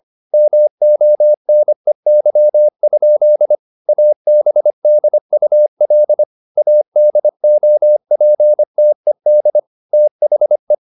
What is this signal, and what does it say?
Morse code, heard as a keyed tone, about 25 wpm, 600 Hz. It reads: MONEY? ABDUL ADOPTED THE